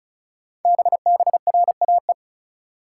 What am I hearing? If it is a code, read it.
Morse code, heard as a keyed tone, about 36 wpm, 690 Hz. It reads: BBRAE
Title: Code Groups - 5 Letters - 35wpm